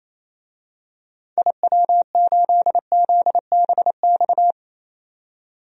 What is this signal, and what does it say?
Morse code, heard as a keyed tone, about 27 wpm, 700 Hz. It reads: IW8ZBX
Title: DXCC-100 Call Signs - 6 Characters - 28wpm